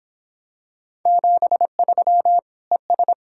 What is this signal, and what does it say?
Morse code, heard as a keyed tone, about 25 wpm, 700 Hz. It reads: 73 ES